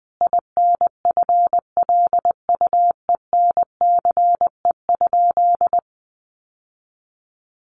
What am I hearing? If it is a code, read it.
Morse code, heard as a keyed tone, about 20 wpm, 700 Hz. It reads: INFLUENCE?
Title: Sentences from Top 1100 Words 20wpm